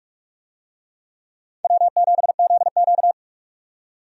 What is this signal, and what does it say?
Morse code, heard as a keyed tone, about 46 wpm, 700 Hz. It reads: W7ZQ
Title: USA Call Signs - 4 Characters - 45wpm